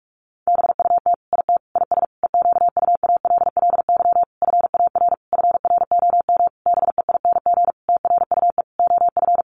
Morse code, heard as a keyed tone, about 44 wpm, 700 Hz. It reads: BUT IT IS EQUALLY FAR FROM BEING TRUE OF
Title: Sentences from Top 1400 Words 45wpm